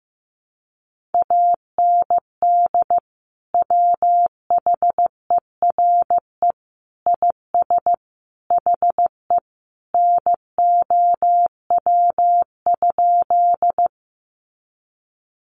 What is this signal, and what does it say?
Morse code, heard as a keyed tone, about 15 wpm, 700 Hz. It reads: AND WHERE IS HE NOW?
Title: Sentences from Top 200 Words 15wpm